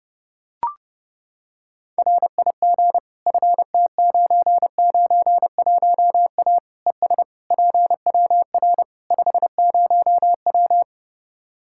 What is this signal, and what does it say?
Morse code, heard as a keyed tone, about 30 wpm, 700 Hz. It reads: RIG FT991A ES PWR 50W
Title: K1USN SST QSO Practice 30wpm